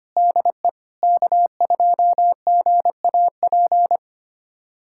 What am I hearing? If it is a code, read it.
Morse code, heard as a keyed tone, about 25 wpm, 700 Hz. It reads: DE K2GAP
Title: USA Callsign DE Callsign - 5 Characters - 25wpm